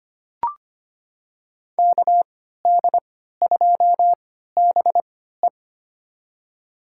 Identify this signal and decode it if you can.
Morse code, heard as a keyed tone, about 25 wpm, spaced as for 14 wpm, 700 Hz. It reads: KD2BE
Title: USA Call Signs - 5 Characters - 3x Character Spacing - 25wpm